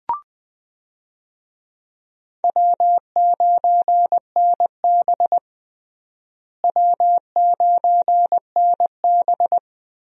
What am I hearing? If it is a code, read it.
Morse code, heard as a keyed tone, about 20 wpm, 700 Hz. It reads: W9NB W9NB
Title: USA Call Signs - 4 Characters - Repeated 2x - 20wpm